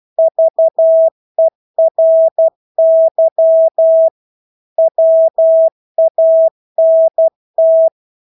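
Morse code, heard as a keyed tone, about 12 wpm, 650 Hz. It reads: VERY WANT
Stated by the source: Morse practice of English words at 12 wpm.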